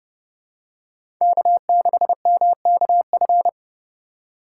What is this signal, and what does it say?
Morse code, heard as a keyed tone, about 32 wpm, 700 Hz. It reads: K6MKF
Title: K1USN SST QSO Practice - 30wpm